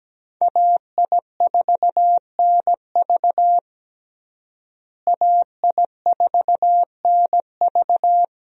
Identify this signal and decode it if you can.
Morse code, heard as a keyed tone, about 17 wpm, 700 Hz. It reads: AI4NV AI4NV
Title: USA Call Signs - 5 Characters - Repeated 2x - 17wpm